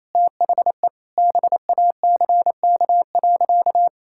Morse code, heard as a keyed tone, about 27 wpm, 700 Hz. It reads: THE BACK.
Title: Sentences from Top 1000 Words - Review Part 5 - 28wpm